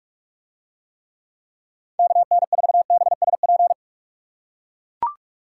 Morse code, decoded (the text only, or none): KN4BSP